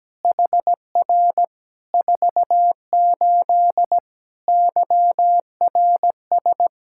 Morse code, heard as a keyed tone, about 17 wpm, 700 Hz. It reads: HR 48 YRS
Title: Practice QSO - Exchange 3 - More Info - 17wpm